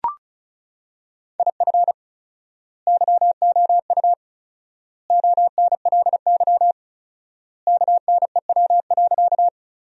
Morse code, heard as a keyed tone, about 36 wpm, 700 Hz. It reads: IF YOU ONLY KNEW.
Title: 4x Word-Spacing - Sentences from Top 400 Words - 35wpm